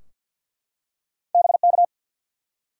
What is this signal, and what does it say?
Morse code, heard as a keyed tone, about 52 wpm, 700 Hz. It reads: BK
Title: Top 90 QSO Elements 50wpm